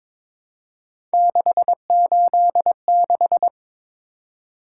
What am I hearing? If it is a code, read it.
Morse code, heard as a keyed tone, about 22 wpm, 700 Hz. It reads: 686
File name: practice / 22wpm / Three Letter-Number Combinations - 22wpm